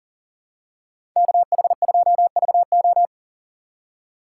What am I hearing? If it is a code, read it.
Morse code, heard as a keyed tone, about 40 wpm, 690 Hz. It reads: KH2VO
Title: USA Call Signs - 5 Characters - 3x Character Spacing - 40wpm